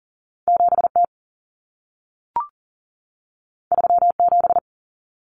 Morse code, heard as a keyed tone, about 40 wpm, 700 Hz. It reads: 7T 37
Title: Common Cut Numbers - 2-Digit 40wpm